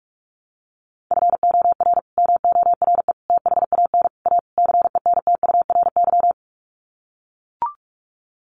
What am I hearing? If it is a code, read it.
Morse code, heard as a keyed tone, about 44 wpm, 690 Hz. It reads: FOR MORE THAN A CENTURY
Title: Sets of 5 Words Encore2 45wpm